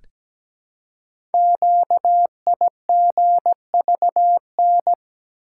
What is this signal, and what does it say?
Morse code, heard as a keyed tone, about 17 wpm, 700 Hz. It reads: QIGVN
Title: Code Groups - 5 Letters - 17wpm